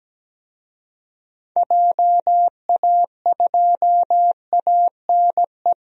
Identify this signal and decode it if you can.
Morse code, heard as a keyed tone, about 17 wpm, 700 Hz. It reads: JA2ANE
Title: DXCC-100 Call Signs - 6 Characters - Repeated 3x - 17wpm